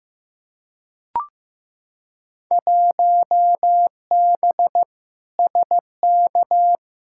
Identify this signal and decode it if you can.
Morse code, heard as a keyed tone, about 15 wpm, 700 Hz. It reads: 1B SK
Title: Field Day Exchanges Encore - 15wpm